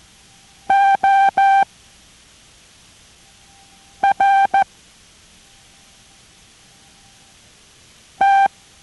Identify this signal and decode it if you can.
Morse code, heard as a keyed tone, about 14 wpm, 790 Hz. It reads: O R T